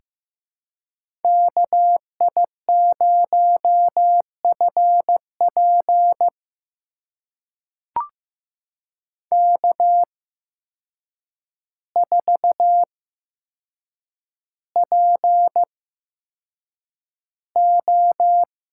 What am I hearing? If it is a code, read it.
Morse code, heard as a keyed tone, about 15 wpm, spaced as for 7 wpm, 700 Hz. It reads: KI0FP K4PO
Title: USA Call Signs - 5 Characters - 8x Character Spacing - 15wpm